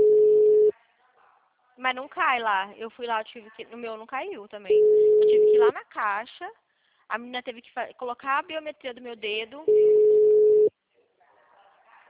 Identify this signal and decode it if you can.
Ringback tone (Europe, 425 Hz), with speech.